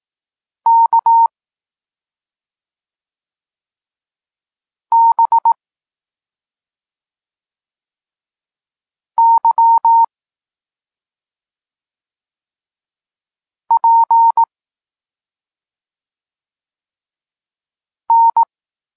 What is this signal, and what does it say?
Morse code, heard as a keyed tone, about 18 wpm, 920 Hz. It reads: K B Y P N